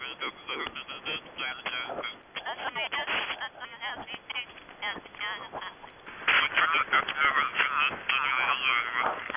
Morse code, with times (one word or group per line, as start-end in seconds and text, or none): none